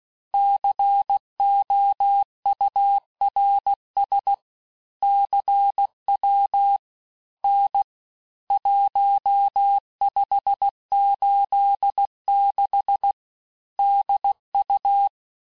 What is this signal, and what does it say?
Morse code, heard as a keyed tone, about 16 wpm, 800 Hz. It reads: COURS CW N 1586 DU
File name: cours-du-lundi files / Cours-1586-texte-chiffre-vitesse-15mpm